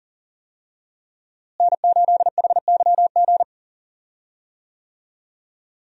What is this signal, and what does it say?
Morse code, heard as a keyed tone, about 40 wpm, 700 Hz. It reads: N8HYG
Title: USA Call Signs - 5 Characters - Repeated 2x - 40wpm